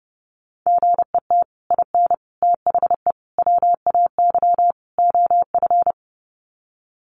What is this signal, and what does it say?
Morse code, heard as a keyed tone, about 30 wpm, 700 Hz. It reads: GET IN THE WAY OF
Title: Sets of 5 Words Encore 30wpm